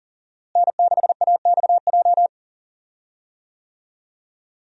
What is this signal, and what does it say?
Morse code, heard as a keyed tone, about 40 wpm, 700 Hz. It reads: N6AXJ
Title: USA Call Signs - 5 Characters - Repeated 2x - 40wpm